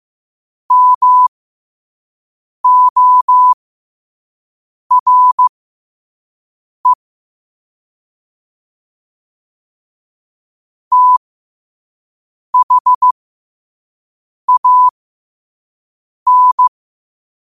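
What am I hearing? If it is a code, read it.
Morse code, heard as a keyed tone, about 15 wpm, spaced as for 5 wpm, 1000 Hz. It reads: MORE THAN
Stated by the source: Morse code sent at 5 wpm.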